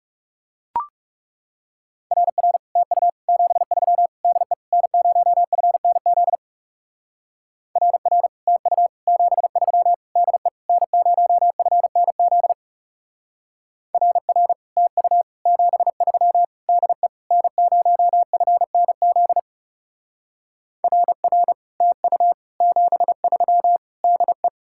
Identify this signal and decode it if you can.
Morse code, heard as a keyed tone, about 36 wpm, 700 Hz. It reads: RR TU 73 DE N0FNZ RR TU 73 DE N0FNZ RR TU 73 DE N0FNZ RR TU 73 DE
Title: Speed-Racing Common QSO Phrases - 30wpm